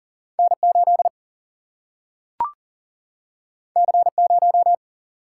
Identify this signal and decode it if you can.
Morse code, heard as a keyed tone, about 40 wpm, 700 Hz. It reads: N 8 C 0
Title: Single-Letter Single-Number 40wpm